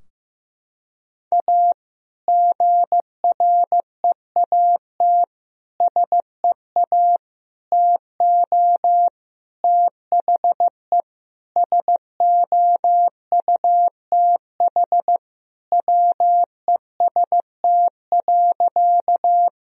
Morse code, heard as a keyed tone, about 15 wpm, 700 Hz. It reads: A GREAT SEA TO THE SOUTH WEST.
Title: Sentences from Top 1000 Words - Review Part 2 - 15wpm